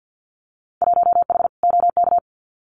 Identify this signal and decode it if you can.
Morse code, heard as a keyed tone, about 50 wpm, 700 Hz. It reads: 1H OK
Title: Winter Field Day Exchanges 50wpm